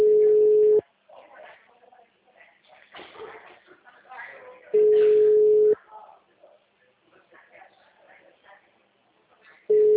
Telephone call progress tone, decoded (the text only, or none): ringback tone